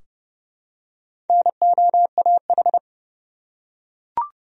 Morse code, heard as a keyed tone, about 31 wpm, 700 Hz. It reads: NOAH
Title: USA Common Names - Max 4 Letters - 30wpm